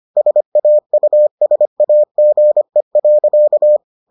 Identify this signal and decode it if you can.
Morse code, heard as a keyed tone, about 25 wpm, 600 Hz. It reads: SAUSAGE.